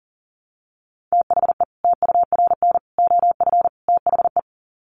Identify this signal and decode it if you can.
Morse code, heard as a keyed tone, about 38 wpm, 700 Hz. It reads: THE TURN OF THE
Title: Sets of 4 Words 40wpm